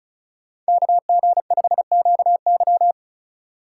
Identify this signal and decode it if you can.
Morse code, heard as a keyed tone, about 36 wpm, 700 Hz. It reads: KG5QY
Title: USA Call Signs - 5 Characters - 3x Character Spacing - 35wpm